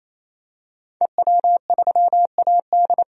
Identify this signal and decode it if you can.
Morse code, heard as a keyed tone, about 27 wpm, 700 Hz. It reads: EW3AD
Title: DXCC-100 Call Signs - 5 Characters - 28wpm